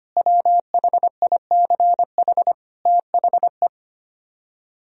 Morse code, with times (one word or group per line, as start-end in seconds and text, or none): 0.2-2.5 WHICH
2.9-3.7 THE